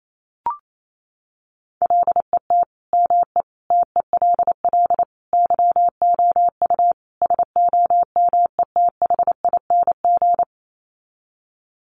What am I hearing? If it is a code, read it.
Morse code, heard as a keyed tone, about 28 wpm, 700 Hz. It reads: LET ME TELL YOU SOMETHING
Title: Sets of 5 Words 28wpm